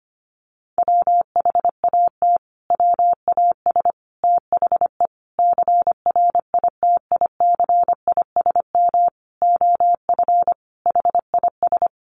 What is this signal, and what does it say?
Morse code, heard as a keyed tone, about 25 wpm, 700 Hz. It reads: WHAT WAS THE CRITICISM OF HIS
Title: Sentences from Top 1300 Words 25wpm